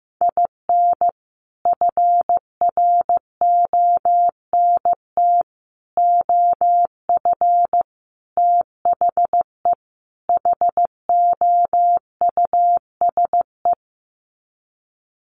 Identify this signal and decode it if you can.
Morse code, heard as a keyed tone, about 15 wpm, 700 Hz. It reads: IN FRONT OF THE HOUSE
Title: Sets of 5 Words Encore2 15wpm